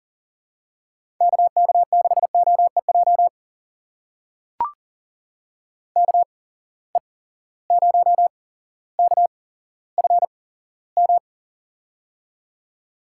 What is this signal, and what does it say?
Morse code, heard as a keyed tone, about 40 wpm, spaced as for 17 wpm, 700 Hz. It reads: KK6OEJ KE0KFM
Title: USA Call Signs - 6 Characters - 8x Character Spacing - 40wpm